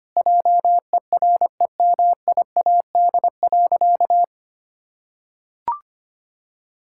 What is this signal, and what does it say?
Morse code, heard as a keyed tone, about 25 wpm, 700 Hz. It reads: JEREMIAD.
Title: Pangram Challenge v2 - 25wpm